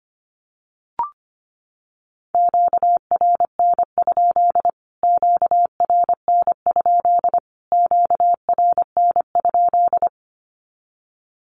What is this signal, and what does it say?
Morse code, heard as a keyed tone, about 25 wpm, 700 Hz. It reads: QRN? QRN? QRN?
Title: Q-Codes Repeated 3x 25wpm